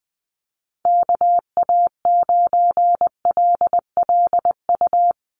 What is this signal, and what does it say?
Morse code, heard as a keyed tone, about 20 wpm, 700 Hz. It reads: KA9LLU
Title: USA Call Signs 20wpm